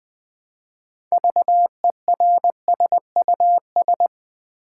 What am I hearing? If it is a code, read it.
Morse code, heard as a keyed tone, about 20 wpm, 700 Hz. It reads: VERSUS
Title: 6 Letter Words - First 3 Letters Given - 20wpm